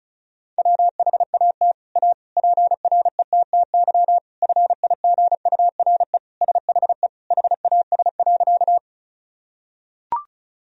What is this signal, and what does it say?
Morse code, heard as a keyed tone, about 34 wpm, 700 Hz. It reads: WHAT A PRETTY FIGURE SHE HAS.
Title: Sentences from Top 1000 Words - Review Part 2 - 35wpm